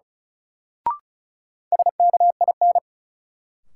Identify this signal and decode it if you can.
Morse code, heard as a keyed tone, about 36 wpm, 700 Hz. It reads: SKIN
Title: Sentences from Top 1300 Words - New Word Drill - 35wpm